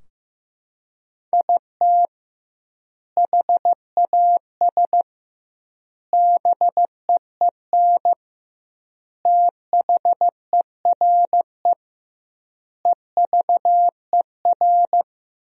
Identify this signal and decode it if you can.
Morse code, heard as a keyed tone, about 15 wpm, 700 Hz. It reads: IT HAS BEEN THERE EVER
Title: 2x Word-Spacing - Sentences from Top 400 Words - 15wpm